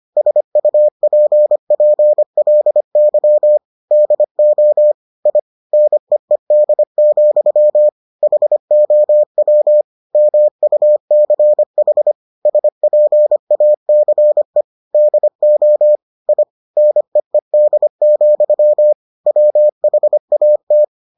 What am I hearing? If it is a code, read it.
Morse code, heard as a keyed tone, about 25 wpm, 600 Hz. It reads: SUPPLY DO I NEED, HOW MUCH SPACE DO I NEED, WHAT